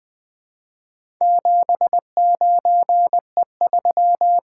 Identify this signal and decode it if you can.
Morse code, heard as a keyed tone, about 20 wpm, 700 Hz. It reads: 79E3
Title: Four Letter-Number Combinations - 20wpm